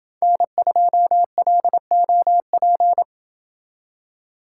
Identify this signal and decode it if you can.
Morse code, heard as a keyed tone, about 28 wpm, 700 Hz. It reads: N2LOP